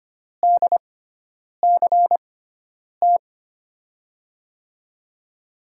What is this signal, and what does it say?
Morse code, heard as a keyed tone, about 25 wpm, spaced as for 9 wpm, 700 Hz. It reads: DCT
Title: USA Call Signs - 6 Characters - 6x Character Spacing - 25wpm